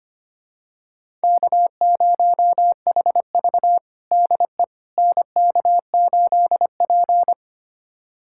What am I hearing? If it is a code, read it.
Morse code, heard as a keyed tone, about 25 wpm, 700 Hz. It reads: K0HV DE NK8P
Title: USA Callsign DE Callsign - 4 Characters - 25wpm